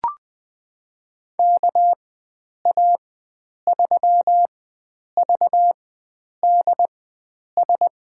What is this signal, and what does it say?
Morse code, heard as a keyed tone, about 20 wpm, spaced as for 9 wpm, 700 Hz. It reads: KA3VDS
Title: USA Call Signs - 6 Characters - 4x Character Spacing - 20wpm